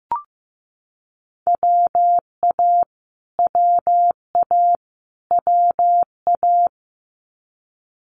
Morse code, heard as a keyed tone, about 15 wpm, 700 Hz. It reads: WA WA WA
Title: Multi-letter Phonemes Repeated 3x 15wpm